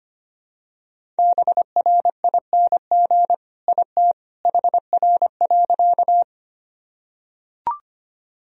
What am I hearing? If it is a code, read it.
Morse code, heard as a keyed tone, about 25 wpm, 700 Hz. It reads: BRING IT HR.
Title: Sentences from Top 500 Words and Abbreviations 25wpm